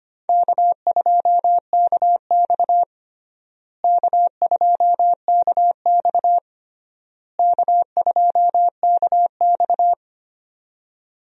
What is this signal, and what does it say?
Morse code, heard as a keyed tone, about 25 wpm, 700 Hz. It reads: K2KX K2KX K2KX
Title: USA Call Signs - 4 Characters - Repeated 3x - 25wpm